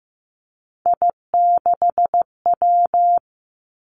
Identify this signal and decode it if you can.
Morse code, heard as a keyed tone, about 15 wpm, 700 Hz. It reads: I6W
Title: Three Challenging Letter-Number Combinations 15wpm